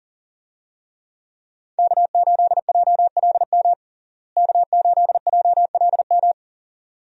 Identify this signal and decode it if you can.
Morse code, heard as a keyed tone, about 40 wpm, 700 Hz. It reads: K8JLM K8JLM
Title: USA Call Signs - 5 Characters - Repeated 2x - 40wpm